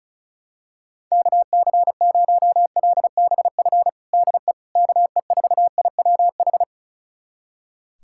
Morse code, heard as a keyed tone, about 34 wpm, 700 Hz. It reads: KC0LBF DE KE4IWH
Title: USA Callsign DE Callsign - 6 Characters - 35wpm